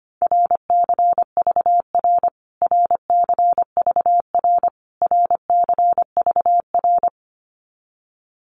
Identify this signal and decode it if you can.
Morse code, heard as a keyed tone, about 25 wpm, 700 Hz. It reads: RCVR RCVR RCVR
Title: Abbreviations Repeated 3x - 25wpm